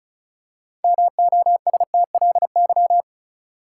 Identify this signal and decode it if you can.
Morse code, heard as a keyed tone, about 34 wpm, 700 Hz. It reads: MOSTLY
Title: Six Letter Words - 35wpm